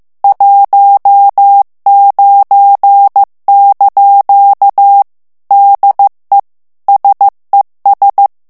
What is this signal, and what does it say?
Morse code, heard as a keyed tone, about 15 wpm, 800 Hz. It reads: 19) DE SES
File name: morse mp3 / FRANCAIS TEXTE CLAIR 1 VITESSE 14 MOTS MINUTES